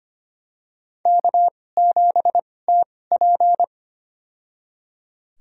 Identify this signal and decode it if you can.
Morse code, heard as a keyed tone, about 25 wpm, spaced as for 18 wpm, 700 Hz. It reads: K7TP
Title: USA Call Signs - 4 Characters - 2x Character Spacing - 25wpm